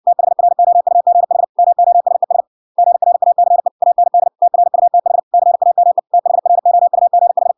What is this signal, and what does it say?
Morse code, heard as a keyed tone, about 60 wpm, 700 Hz. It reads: THROUGH MOSES GRACE AND TRUTH CAME THROUGH